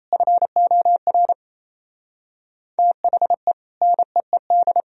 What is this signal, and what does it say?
Morse code, heard as a keyed tone, about 35 wpm, 700 Hz. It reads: FOR THE NEED